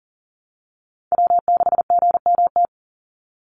Morse code, heard as a keyed tone, about 40 wpm, 700 Hz. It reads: W6GMT
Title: Call Signs - 40wpm